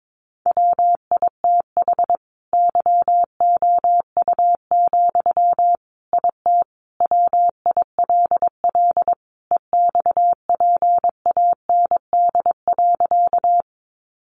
Morse code, heard as a keyed tone, about 22 wpm, 700 Hz. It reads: WITH YOU, IT WILL EXPAND.
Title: Sentences from Top 1400 Words 22wpm